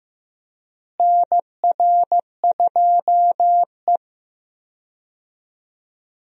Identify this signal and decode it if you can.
Morse code, heard as a keyed tone, about 15 wpm, 700 Hz. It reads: NR2E
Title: USA Call Signs - 4 Characters - Repeated 3x - 15wpm